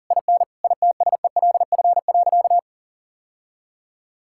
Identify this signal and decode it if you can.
Morse code, heard as a keyed tone, about 40 wpm, 700 Hz. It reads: IN ITSELF.
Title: Sentences from Top 1000 Words - Review Part 5 - 40wpm